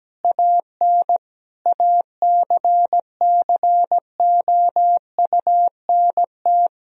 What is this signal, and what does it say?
Morse code, heard as a keyed tone, about 17 wpm, 700 Hz. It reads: AN ACCOUNT
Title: Sentences from Top 1000 Words - Review Part 4 - 17wpm